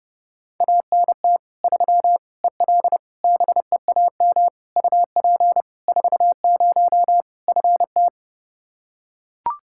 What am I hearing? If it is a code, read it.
Morse code, heard as a keyed tone, about 30 wpm, 700 Hz. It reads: ANT 3 EL BEAM UP 40 FT
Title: K1USN SST QSO Practice 30wpm